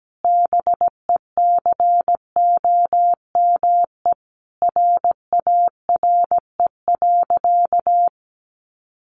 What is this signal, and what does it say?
Morse code, heard as a keyed tone, about 17 wpm, 700 Hz. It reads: BECOME RARE.
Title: Sentences from Top 1100 Words 17wpm